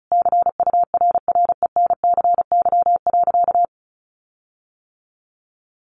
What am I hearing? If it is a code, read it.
Morse code, heard as a keyed tone, about 34 wpm, 700 Hz. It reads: CURRENCY.
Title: Sentences from Top 1500 Words 35wpm